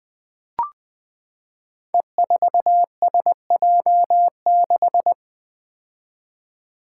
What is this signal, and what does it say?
Morse code, heard as a keyed tone, about 20 wpm, 700 Hz. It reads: E4SJ6
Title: Five Challenging Letter-Number Combinations 20wpm